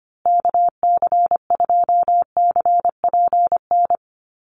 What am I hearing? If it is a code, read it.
Morse code, heard as a keyed tone, about 25 wpm, 700 Hz. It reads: KC2CPN
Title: USA Call Signs Repeated 3x 25wpm